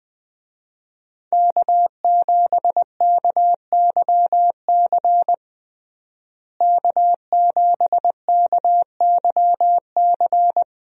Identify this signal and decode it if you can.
Morse code, heard as a keyed tone, about 21 wpm, 700 Hz. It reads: K7KYC K7KYC
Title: USA Call Signs - 5 Characters - Repeated 2x - 20wpm